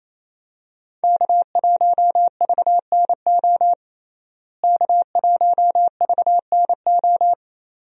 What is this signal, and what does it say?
Morse code, heard as a keyed tone, about 28 wpm, 700 Hz. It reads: K1VNO K1VNO
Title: USA Call Signs - 5 Characters - Repeated 2x - 28wpm